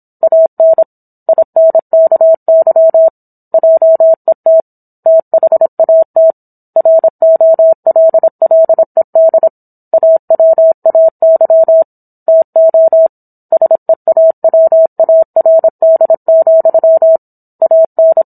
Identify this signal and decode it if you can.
Morse code, heard as a keyed tone, about 27 wpm, 650 Hz. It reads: AN INKY JET THAT ROLLED AWAY TO SEAWARD, AN